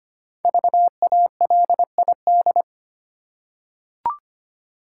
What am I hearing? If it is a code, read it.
Morse code, heard as a keyed tone, about 25 wpm, 700 Hz. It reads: VALID
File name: 5 Letter Words - First 3 Letters Given - 25wpm